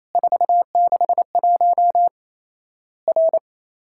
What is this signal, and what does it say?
Morse code, heard as a keyed tone, about 28 wpm, 700 Hz. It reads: 461 R
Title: Worked All Europe DX Contest - QTC Exchange - 28wpm